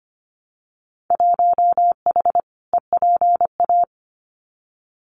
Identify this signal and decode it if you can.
Morse code, heard as a keyed tone, about 25 wpm, 700 Hz. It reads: 1H EPA
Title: Winter Field Day Exchanges 25wpm